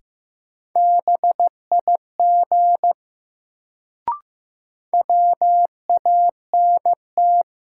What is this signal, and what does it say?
Morse code, heard as a keyed tone, about 15 wpm, 700 Hz. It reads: BIG WANT
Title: Sentences from Top 200 Words - New Word Drill - 15wpm